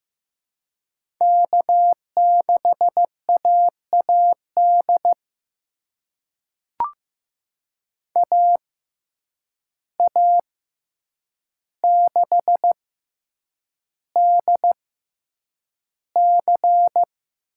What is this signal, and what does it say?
Morse code, heard as a keyed tone, about 15 wpm, spaced as for 8 wpm, 700 Hz. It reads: K6AAD AA6DC
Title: USA Call Signs - 6x Character Spacing - 15wpm